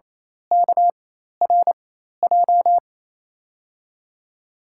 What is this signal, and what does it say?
Morse code, heard as a keyed tone, about 28 wpm, spaced as for 13 wpm, 700 Hz. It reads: KRJ
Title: USA Call Signs - 6 Characters - 4x Character Spacing - 28wpm